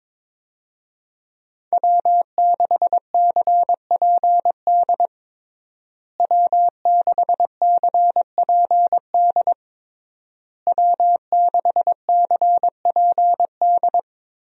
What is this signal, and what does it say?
Morse code, heard as a keyed tone, about 22 wpm, 700 Hz. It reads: W6CPD W6CPD W6CPD
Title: USA Call Signs - 5 Characters - Repeated 3x - 22wpm